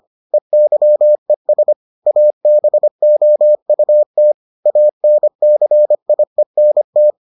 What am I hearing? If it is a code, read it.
Morse code, heard as a keyed tone, about 25 wpm, 600 Hz. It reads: EYES ABOUT ANCIENT